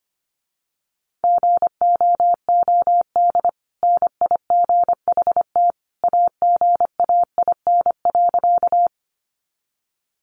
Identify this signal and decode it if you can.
Morse code, heard as a keyed tone, about 25 wpm, 700 Hz. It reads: GOOD NIGHT AGAIN.
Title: Sentences from Top 300 Words 25wpm